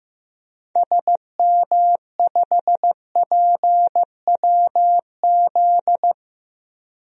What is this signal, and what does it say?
Morse code, heard as a keyed tone, about 16 wpm, 700 Hz. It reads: SM5PWZ
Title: DXCC-100 Call Signs - 6 Characters - Repeated 3x - 15wpm